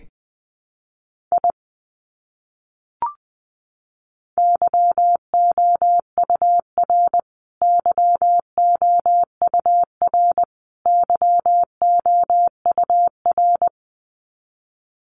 Morse code, heard as a keyed tone, about 20 wpm, 700 Hz. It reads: I YOUR YOUR YOUR
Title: Top 200 Words Repeated 20wpm